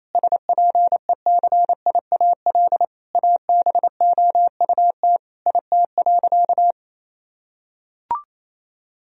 Morse code, heard as a keyed tone, about 28 wpm, 700 Hz. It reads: SPECIAL ABOUT IT.
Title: Sentences from Top 1000 Words - Review Part 3 - 28wpm